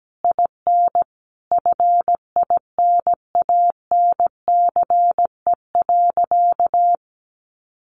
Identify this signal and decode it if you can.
Morse code, heard as a keyed tone, about 17 wpm, 700 Hz. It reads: IN FINANCE.
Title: Sentences from Top 1500 Words 17wpm